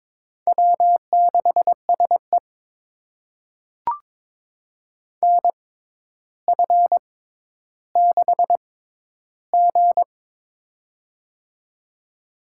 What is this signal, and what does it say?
Morse code, heard as a keyed tone, about 22 wpm, spaced as for 11 wpm, 700 Hz. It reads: W6SE NF6G
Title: USA Call Signs - 4 Characters - 6x Character Spacing - 22wpm